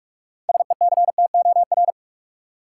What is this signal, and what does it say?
Morse code, heard as a keyed tone, about 42 wpm, 700 Hz. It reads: SECTOR
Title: Six Letter Words - 45wpm